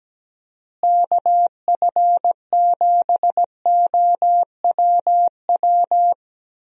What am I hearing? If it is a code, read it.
Morse code, heard as a keyed tone, about 17 wpm, 700 Hz. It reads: KF7OWW
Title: USA Call Signs - 6 Characters - Repeated 3x - 17wpm